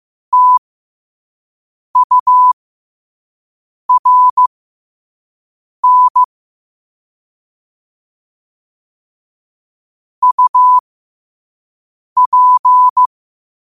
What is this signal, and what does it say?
Morse code, heard as a keyed tone, about 15 wpm, spaced as for 5 wpm, 1000 Hz. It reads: TURN UP
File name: qotdfiles / 5wpm